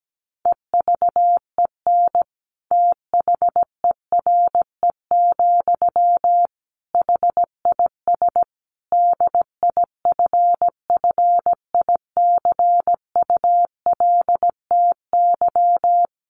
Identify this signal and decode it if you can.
Morse code, heard as a keyed tone, about 17 wpm, 700 Hz. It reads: EVEN THERE, HIS DIFFICULTY
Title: Sentences from Top 1500 Words 17wpm